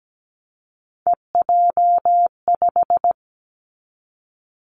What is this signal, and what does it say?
Morse code, heard as a keyed tone, about 17 wpm, 700 Hz. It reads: EJ5
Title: Three Challenging Letter-Number Combinations 17wpm